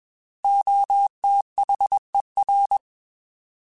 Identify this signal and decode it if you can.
Morse code, heard as a keyed tone, about 21 wpm, 800 Hz. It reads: OTHER
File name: Alphabetical / Alphabetical mp3s / Os @20